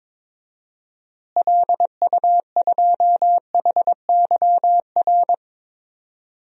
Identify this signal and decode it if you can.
Morse code, heard as a keyed tone, about 22 wpm, 700 Hz. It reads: LU2HYR
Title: DXCC-100 Call Signs - 6 Characters - Repeated 3x - 22wpm